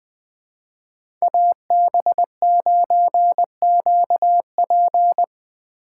Morse code, heard as a keyed tone, about 20 wpm, 700 Hz. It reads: AB9QP
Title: USA Call Signs - 5 Characters - 2x Character Spacing - 20wpm